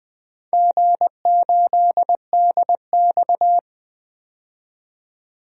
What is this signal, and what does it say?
Morse code, heard as a keyed tone, about 20 wpm, 700 Hz. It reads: G8DX
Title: DXCC-100 Call Signs - 4 Characters - 20wpm